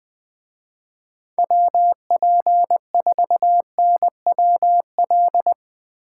Morse code, heard as a keyed tone, about 20 wpm, 700 Hz. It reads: WP4NWL
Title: Speed Racing - USA 2x3 Call Signs 17wpm